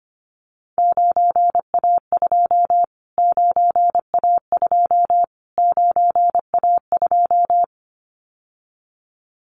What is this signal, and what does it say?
Morse code, heard as a keyed tone, about 25 wpm, 700 Hz. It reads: 9A2 9A2 9A2
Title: Callsign Prefixes for Top 25 DXCC - Repeated 3x - 25wpm